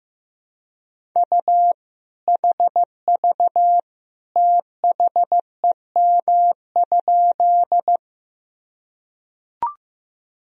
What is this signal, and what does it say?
Morse code, heard as a keyed tone, about 16 wpm, 700 Hz. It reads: U HV THEM?
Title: Sentences from Top 100 Words and Abbreviations 15wpm